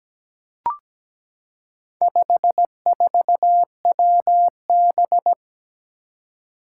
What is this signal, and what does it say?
Morse code, heard as a keyed tone, about 17 wpm, 700 Hz. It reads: 54WB
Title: Four Challenging Letter-Number Combinations 17wpm